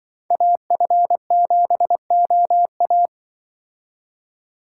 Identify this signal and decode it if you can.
Morse code, heard as a keyed tone, about 25 wpm, 700 Hz. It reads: AF7OA
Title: Speed Racing - USA Call Signs 20wpm